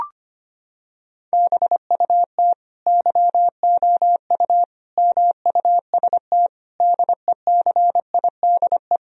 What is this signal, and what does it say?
Morse code, heard as a keyed tone, about 25 wpm, 700 Hz. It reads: BUT YOU MUST DECIDE
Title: Sentences from Top 1000 Words - Review Part 3 - 25wpm